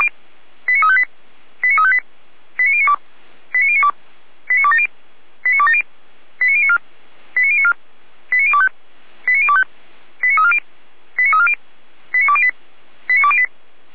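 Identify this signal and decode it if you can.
Five-tone selective call (ZVEI), heard as quick runs of five tones, tones of about 70 ms. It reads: 89358 89358 89092 89092 89270 89270 89094 89094 89025 89025 89340 89340 89208 89208
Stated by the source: ZVEI five-tone selective call received by SDR, Feuerwehr Orlamünde (fire brigade).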